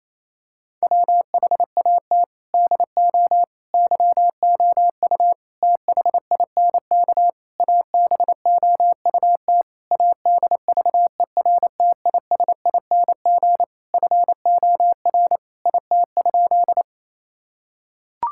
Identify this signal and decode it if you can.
Morse code, heard as a keyed tone, about 27 wpm, 700 Hz. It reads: WHAT DO YOU THINK ABOUT ADVERTISING FOR IT?
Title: Sentences from Top 1000 Words - Review Part 6 - 28wpm